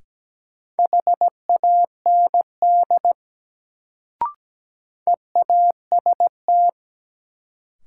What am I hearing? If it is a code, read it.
Morse code, heard as a keyed tone, about 17 wpm, 700 Hz. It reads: HAND EAST
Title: Sets of 3 Words - New Word Drill - 17wpm